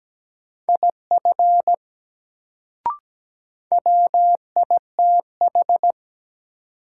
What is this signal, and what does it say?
Morse code, heard as a keyed tone, about 17 wpm, 700 Hz. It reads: IF WITH
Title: Sets of 2 Words - New Word Drill - 17wpm